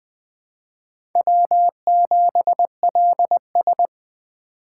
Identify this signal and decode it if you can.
Morse code, heard as a keyed tone, about 20 wpm, 700 Hz. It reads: W7LS
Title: USA Call Signs - 4 Characters - 2x Character Spacing - 20wpm